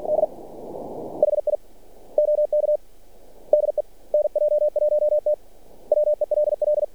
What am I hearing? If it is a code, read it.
Morse code, heard as a keyed tone, about 45 wpm, 600 Hz. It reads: D HI OK DE NJ1T WELL
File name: archived / CRSnet